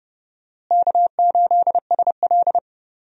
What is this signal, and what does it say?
Morse code, heard as a keyed tone, about 30 wpm, 700 Hz. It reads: K8SL
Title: USA Call Signs - 4 Characters - Repeated 2x - 30wpm